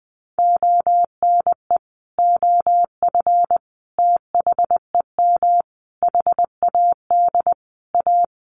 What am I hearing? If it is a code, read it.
Morse code, heard as a keyed tone, about 20 wpm, 700 Hz. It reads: ONE OF THEM HAD A
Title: Sentences from Top 1300 Words 20wpm